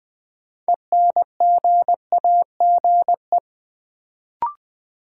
Morse code, heard as a keyed tone, about 20 wpm, 700 Hz. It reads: ENGAGE
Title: Sentences from Top 1500 Words - New Word Drill Alt - 20wpm